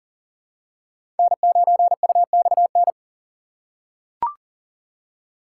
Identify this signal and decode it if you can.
Morse code, heard as a keyed tone, about 40 wpm, 700 Hz. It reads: N9UXN
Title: USA Call Signs - 5 Characters - 4x Character Spacing - 40wpm